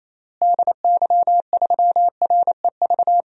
Morse code, heard as a keyed tone, about 28 wpm, 700 Hz. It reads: DY3REV
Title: DXCC-100 Call Signs - 6 Characters - Repeated 3x - 28wpm